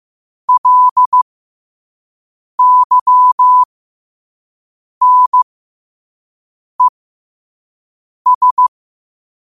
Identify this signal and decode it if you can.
Morse code, heard as a keyed tone, about 15 wpm, spaced as for 5 wpm, 1000 Hz. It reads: LYNES
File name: qotdfiles / 5wpm